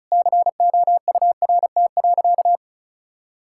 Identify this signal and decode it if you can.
Morse code, heard as a keyed tone, about 36 wpm, 700 Hz. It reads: COURT.
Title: Sentences from Top 1600 Words 35wpm